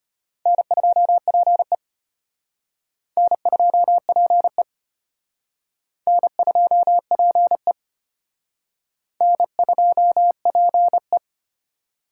Morse code, heard as a keyed tone, about 31 wpm, 700 Hz. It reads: N2PE N2PE N2PE N2PE
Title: Speed Racing - USA 1x2 Call Signs 25wpm